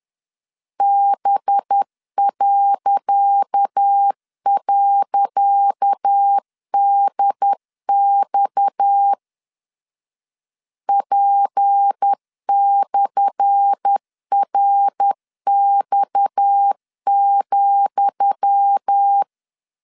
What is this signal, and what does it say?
Morse code, heard as a keyed tone, about 10 wpm, 800 Hz. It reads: B..DX P/RX,
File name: LEZIONI / 10 SIGNS / 10D COMA 10wpm 3013